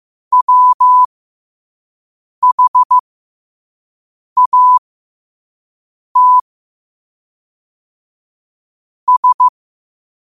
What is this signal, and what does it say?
Morse code, heard as a keyed tone, about 15 wpm, spaced as for 5 wpm, 1000 Hz. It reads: WHAT S
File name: 5wpm